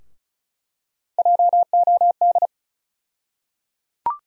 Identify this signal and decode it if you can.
Morse code, heard as a keyed tone, about 36 wpm, 700 Hz. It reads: JOD
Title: USA Call Sign Postfixes - 3 Letters 35wpm